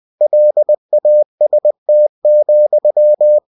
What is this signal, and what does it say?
Morse code, heard as a keyed tone, about 20 wpm, 600 Hz. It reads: LAST,
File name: cw-books / winnie-the-pooh / winnie-20wpm-page-0015